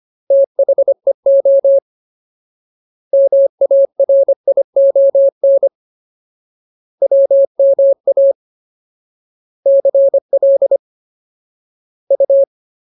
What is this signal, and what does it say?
Morse code, heard as a keyed tone, about 25 wpm, 550 Hz. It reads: THEO MARION WMA CL U
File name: Ultimate Word List / 25WPM Ultimate